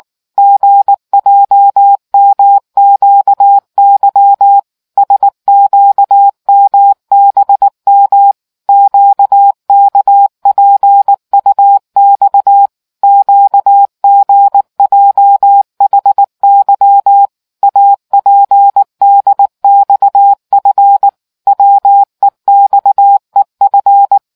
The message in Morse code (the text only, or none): GJMQY SQMBM QKPUX QGJHY APDXF WEXEF